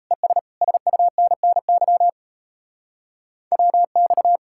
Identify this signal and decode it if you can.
Morse code, heard as a keyed tone, about 34 wpm, 700 Hz. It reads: ES SUNNY WX
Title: Speed-Racing Common QSO Phrases Encore - 28wpm